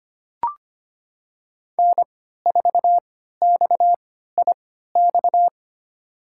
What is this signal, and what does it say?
Morse code, heard as a keyed tone, about 25 wpm, spaced as for 14 wpm, 700 Hz. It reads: N4XIX
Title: USA Call Signs - 5 Characters - 3x Character Spacing - 25wpm